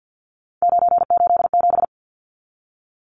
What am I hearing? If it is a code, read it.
Morse code, heard as a keyed tone, about 50 wpm, 700 Hz. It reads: 987
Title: Numbers - 3-Digit 50wpm